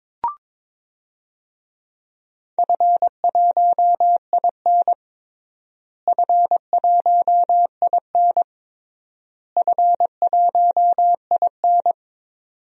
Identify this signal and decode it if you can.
Morse code, heard as a keyed tone, about 22 wpm, 700 Hz. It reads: F1IN F1IN F1IN
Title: DXCC-100 Call Signs - 4 Characters - Repeated 3x - 22wpm